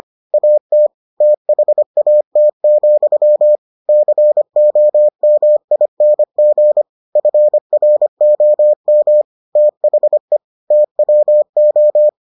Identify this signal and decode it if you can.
Morse code, heard as a keyed tone, about 25 wpm, 600 Hz. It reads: AT THAT, COMING FROM THE TWO